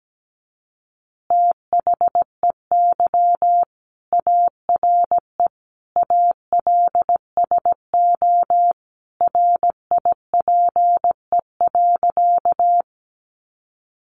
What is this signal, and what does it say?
Morse code, heard as a keyed tone, about 17 wpm, 700 Hz. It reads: THEY ARE ALSO RIPE.
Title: Additional Easy Phrases 17wpm